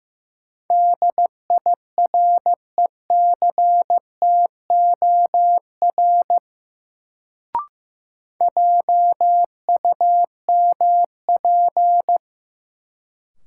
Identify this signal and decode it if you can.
Morse code, heard as a keyed tone, about 15 wpm, 700 Hz. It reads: DIRECTOR JUMP
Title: Sentences from Top 1300 Words - New Word Drill - 15wpm